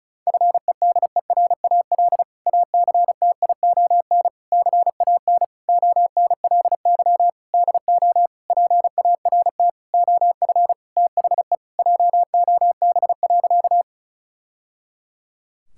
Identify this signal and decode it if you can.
Morse code, heard as a keyed tone, about 34 wpm, 700 Hz. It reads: FEDERAL ACTION CAN ONLY DO PART OF THE JOB.
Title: Sentences from Top 1000 Words - Review Part 5 - 35wpm